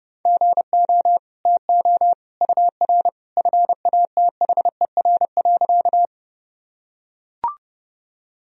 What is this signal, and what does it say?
Morse code, heard as a keyed tone, about 32 wpm, 700 Hz. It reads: GO TO UR FATHER.
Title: Sentences from Top 300 Words and Abbreviations 30wpm